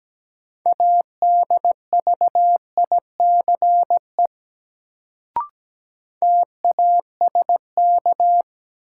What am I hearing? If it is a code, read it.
Morse code, heard as a keyed tone, about 17 wpm, 700 Hz. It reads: ADVICE TASK
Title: Sentences from Top 1100 Words - New Word Drill Alt - 17wpm